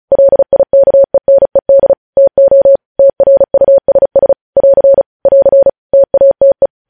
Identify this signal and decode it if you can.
Morse code, heard as a keyed tone, about 36 wpm, 550 Hz. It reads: LIKENED TO TRUSS <AR> <AR> TATE
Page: News headlines in Morse code at 35 wpm